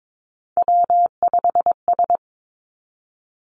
Morse code, heard as a keyed tone, about 22 wpm, 700 Hz. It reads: W5S
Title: Three Challenging Letter-Number Combinations 22wpm